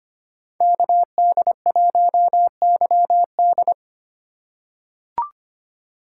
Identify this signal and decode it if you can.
Morse code, heard as a keyed tone, about 25 wpm, 700 Hz. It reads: KD1YD
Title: USA Call Signs - 5 Characters - 3x Character Spacing - 25wpm